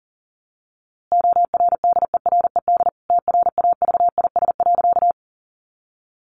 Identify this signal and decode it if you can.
Morse code, heard as a keyed tone, about 40 wpm, 700 Hz. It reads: ORDERED TRAVIS.
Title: A Sound Of Thunder 40wpm